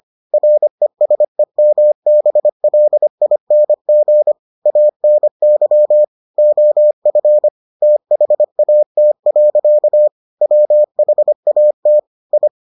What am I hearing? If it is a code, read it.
Morse code, heard as a keyed tone, about 25 wpm, 600 Hz. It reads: RESEMBLING ANY OF THAT. WHAT I